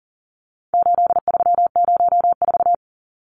Morse code, heard as a keyed tone, about 40 wpm, 700 Hz. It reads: 8304
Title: Numbers - 4-Digit 40wpm